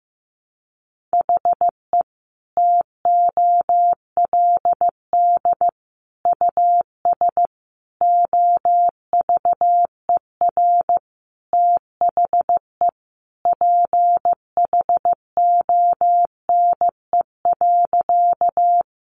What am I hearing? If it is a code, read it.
Morse code, heard as a keyed tone, about 15 wpm, 700 Hz. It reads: HE TOLD US OVER THE PHONE.
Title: Sentences from Top 1100 Words 15wpm